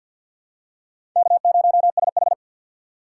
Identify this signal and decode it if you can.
Morse code, heard as a keyed tone, about 48 wpm, 700 Hz. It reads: K0SH
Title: USA Call Signs - 4 Characters - 6x Character Spacing - 50wpm